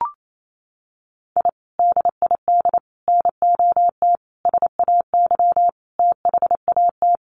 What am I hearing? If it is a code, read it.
Morse code, heard as a keyed tone, about 28 wpm, 700 Hz. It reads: I DID NOT SAY THAT
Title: Sets of 5 Words Encore2 28wpm